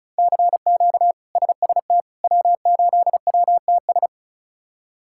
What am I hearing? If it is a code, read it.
Morse code, heard as a keyed tone, about 36 wpm, 700 Hz. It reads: CQ SST W8WTS
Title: K1USN SST QSO Practice - 35wpm